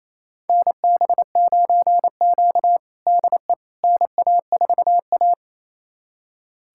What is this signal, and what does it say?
Morse code, heard as a keyed tone, about 28 wpm, 700 Hz. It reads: NB9Q DE NA4A
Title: USA Callsign DE Callsign - 4 Characters - 28wpm